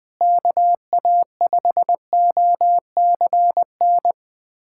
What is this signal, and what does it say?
Morse code, heard as a keyed tone, about 21 wpm, 700 Hz. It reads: KA5OCN
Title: USA Callsign DE Callsign - 6 Characters - 20wpm